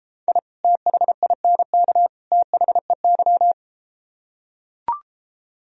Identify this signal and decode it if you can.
Morse code, heard as a keyed tone, about 32 wpm, 700 Hz. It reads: I THINK THEY